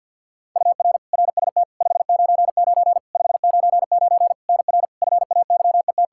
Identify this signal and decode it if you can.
Morse code, heard as a keyed tone, about 50 wpm, 700 Hz. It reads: UR RST 599 599 NR FAYET
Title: Practice QSO - Exchange 1 - Beginning - 50wpm